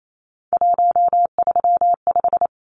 Morse code, heard as a keyed tone, about 28 wpm, 700 Hz. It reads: 135
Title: Numbers - 3-Digit 28wpm